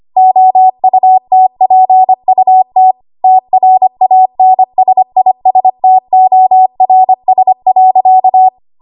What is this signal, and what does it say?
Morse code, heard as a keyed tone, about 25 wpm, 750 Hz. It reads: OUTPUT TRANSISTORS.
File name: Archive / 25 WPM / 25WPM